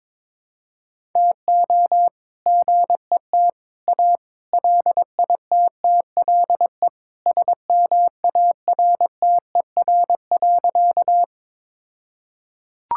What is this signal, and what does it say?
Morse code, heard as a keyed tone, about 22 wpm, 700 Hz. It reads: TO GET A LITTLE SMARTER.
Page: morse-code-ninja-practice-22wpm